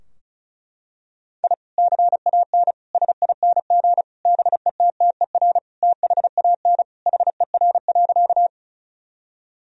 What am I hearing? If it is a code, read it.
Morse code, heard as a keyed tone, about 35 wpm, 700 Hz. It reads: I CAN SING BETTER THAN HER.
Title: Sentences from Top 1000 Words - Review Part 2 - 35wpm